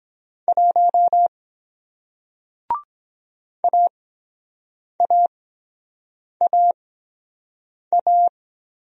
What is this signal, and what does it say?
Morse code, heard as a keyed tone, about 23 wpm, spaced as for 7 wpm, 700 Hz. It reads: 1 AAAA